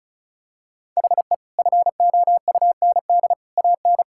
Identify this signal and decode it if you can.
Morse code, heard as a keyed tone, about 34 wpm, 700 Hz. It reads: HE FOUND AN
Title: Sentences from Top 1000 Words - Review Part 6 - 35wpm